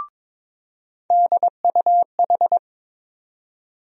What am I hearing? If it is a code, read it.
Morse code, heard as a keyed tone, about 22 wpm, 700 Hz. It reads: DUH
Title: USA Call Sign Postfixes - 3 Letters 22wpm